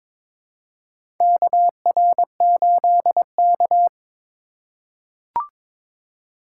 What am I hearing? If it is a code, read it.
Morse code, heard as a keyed tone, about 21 wpm, 700 Hz. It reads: KR8K
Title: USA Call Signs - 4 Characters - 2x Character Spacing - 22wpm